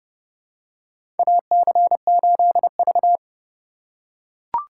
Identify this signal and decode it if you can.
Morse code, heard as a keyed tone, about 30 wpm, 700 Hz. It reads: AC8V
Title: USA Call Signs - 2x Character Spacing - 30wpm